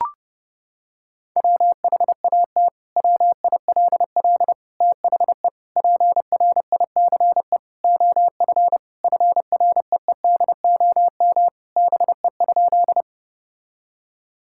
Morse code, heard as a keyed tone, about 30 wpm, 700 Hz. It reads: WHAT WILL THE PRICE OF FREEDOM BE?
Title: Sentences from Top 1000 Words - Review Part 5 - 30wpm